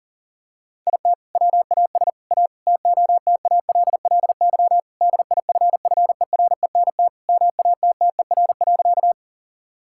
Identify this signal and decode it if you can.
Morse code, heard as a keyed tone, about 40 wpm, 700 Hz. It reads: IT WAS A TOTALLY DIFFERENT MATTER.
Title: Sentences from Top 1600 Words 40wpm